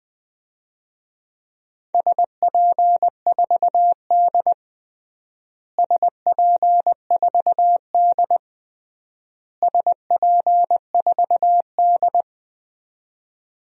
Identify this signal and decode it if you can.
Morse code, heard as a keyed tone, about 20 wpm, 700 Hz. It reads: SP4D SP4D SP4D
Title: DXCC-100 Call Signs - 4 Characters - Repeated 3x - 20wpm